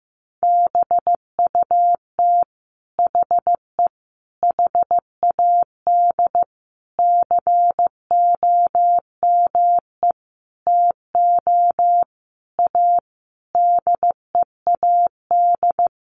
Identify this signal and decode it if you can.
Morse code, heard as a keyed tone, about 15 wpm, 700 Hz. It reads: BUT HE HAD COME TO A DEAD
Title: Sentences from Top 1500 Words 15wpm